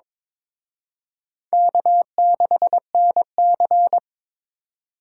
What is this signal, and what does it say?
Morse code, heard as a keyed tone, about 26 wpm, 700 Hz. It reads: K6NC
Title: Speed Racing - USA 1x2 Call Signs 22wpm